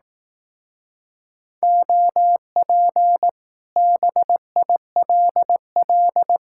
Morse code, heard as a keyed tone, about 18 wpm, 700 Hz. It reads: OP BILL
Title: Speed-Racing Common QSO Phrases Encore - 15wpm